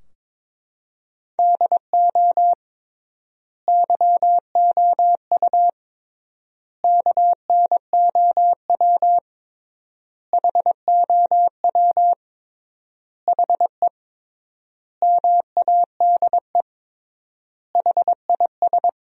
Morse code, heard as a keyed tone, about 22 wpm, 700 Hz. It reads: DO YOU KNOW HOW HE MADE HIS
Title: 3x Word-Spacing - Sentences from Top 300 Words - 22wpm